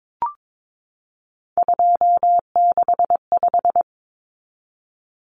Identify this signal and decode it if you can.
Morse code, heard as a keyed tone, about 22 wpm, 700 Hz. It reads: 265
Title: Numbers - 3-Digit 22wpm